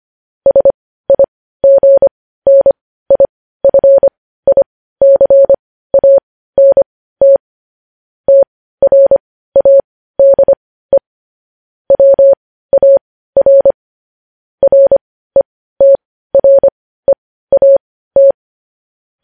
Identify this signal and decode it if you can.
Morse code, heard as a keyed tone, about 25 wpm, spaced as for 15 wpm, 550 Hz. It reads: SIGNIFICANT TRADE WAR RETREAT
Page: news-headlines-in-morse-code-at-15-wpm